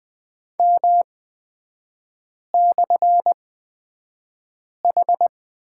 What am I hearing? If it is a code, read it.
Morse code, heard as a keyed tone, about 20 wpm, spaced as for 5 wpm, 690 Hz. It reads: M/H
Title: 229 Lesson 37e - Letter x - Three Characters 20wpm